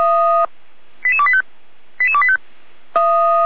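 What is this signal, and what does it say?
Five-tone selective call (ZVEI), heard as quick runs of five tones, tones of about 70 ms. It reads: 88285 88285
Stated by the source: ZVEI five-tone selective call received via SDR, from the Feuerwehr Orlamünde fire brigade audio archive.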